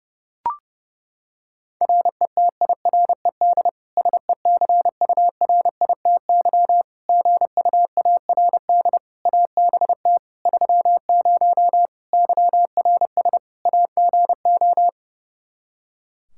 Morse code, heard as a keyed tone, about 30 wpm, 700 Hz. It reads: RETIRED SECURITY GUARD ABT 30 YRS AGO
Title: K1USN SST QSO Practice 30wpm